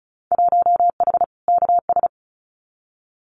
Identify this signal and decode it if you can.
Morse code, heard as a keyed tone, about 34 wpm, 700 Hz. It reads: 1H KS
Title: Winter Field Day Exchanges 35wpm